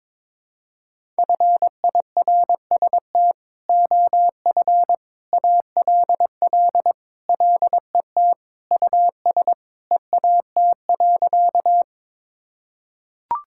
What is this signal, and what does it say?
Morse code, heard as a keyed tone, about 22 wpm, 700 Hz. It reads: FIRST OF ALL LET US EAT.
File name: practice / 22wpm / Sentences with 3-Word Phrases - 22wpm